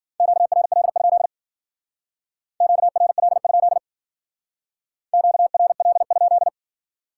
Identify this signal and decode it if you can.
Morse code, heard as a keyed tone, about 52 wpm, 700 Hz. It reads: QRL? QRL? QRL?